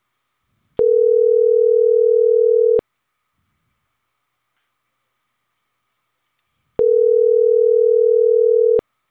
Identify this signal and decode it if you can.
Ringback tone (North America), 440 + 480 Hz.